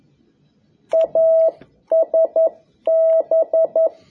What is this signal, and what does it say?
Morse code, heard as a keyed tone, about 11 wpm, 650 Hz. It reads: ASB
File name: jh8cbh / 2021cw